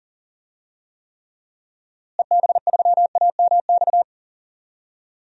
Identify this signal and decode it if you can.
Morse code, heard as a keyed tone, about 37 wpm, 700 Hz. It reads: EB3AMX
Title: DXCC-100 Call Signs - 6 Characters - Repeated 3x - 40wpm